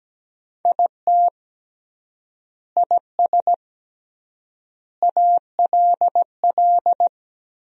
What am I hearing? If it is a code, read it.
Morse code, heard as a keyed tone, about 17 wpm, 700 Hz. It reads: IT IS ALL
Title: 3x Word-Spacing - Sentences from Top 500 Words - 17wpm